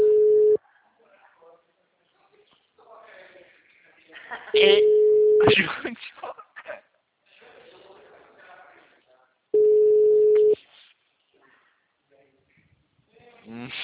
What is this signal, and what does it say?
Ringback tone (Europe, 425 Hz), with speech.
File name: out- -2006- - - .1364